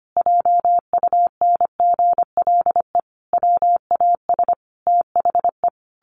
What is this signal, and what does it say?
Morse code, heard as a keyed tone, about 25 wpm, 700 Hz. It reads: JUNGLE WAS THE
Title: A Sound Of Thunder 25wpm